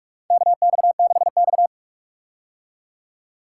Morse code, heard as a keyed tone, about 46 wpm, 700 Hz. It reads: KX6X
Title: K1USN SST QSO Practice - 45wpm